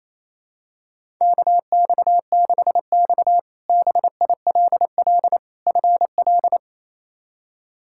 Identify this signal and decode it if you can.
Morse code, heard as a keyed tone, about 28 wpm, 700 Hz. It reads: KX6X BILL FL
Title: K1USN SST QSO Practice - 28wpm